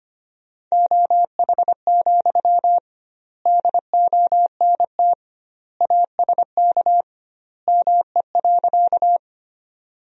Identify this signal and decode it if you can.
Morse code, heard as a keyed tone, about 25 wpm, 700 Hz. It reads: OH, DONT ASK ME.
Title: 2x Word-Spacing - Sentences from Top 500 Words - 25wpm